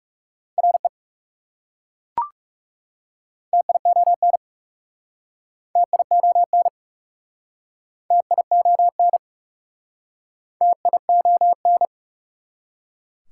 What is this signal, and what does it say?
Morse code, heard as a keyed tone, about 37 wpm, 700 Hz. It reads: RE TION TION TION TION
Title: Speed-Racing Prefixes and Suffixes 30wpm